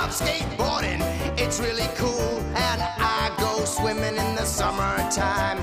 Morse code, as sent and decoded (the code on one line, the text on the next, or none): none